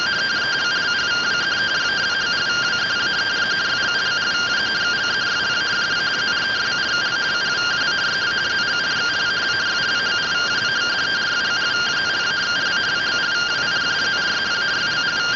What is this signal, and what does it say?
Two-tone teleprinter signal, about 170 Hz shift.